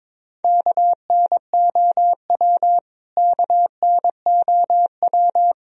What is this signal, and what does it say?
Morse code, heard as a keyed tone, about 22 wpm, 700 Hz. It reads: KNOW KNOW
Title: Top 100 Words Repeated 22wpm